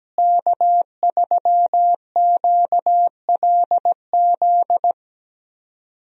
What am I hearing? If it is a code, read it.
Morse code, heard as a keyed tone, about 17 wpm, 700 Hz. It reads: K3QLZ
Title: USA Call Signs - 5 Characters - Repeated 2x - 17wpm